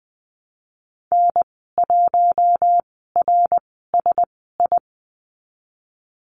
Morse code, heard as a keyed tone, about 20 wpm, spaced as for 15 wpm, 700 Hz. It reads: N1RSI
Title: 055 Lesson 10g - Letter r - Call Signs 20wpm